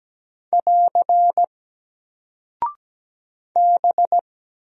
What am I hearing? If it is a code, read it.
Morse code, heard as a keyed tone, about 17 wpm, 700 Hz. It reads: <AR> B
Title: Single Character (Letter:Number:Punctuation:Prosign) Alt 17wpm